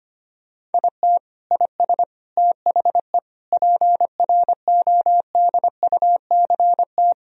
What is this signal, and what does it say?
Morse code, heard as a keyed tone, about 25 wpm, 700 Hz. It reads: IT IS THE PRODUCT
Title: Sentences from Top 1000 Words - Review Part 2 - 25wpm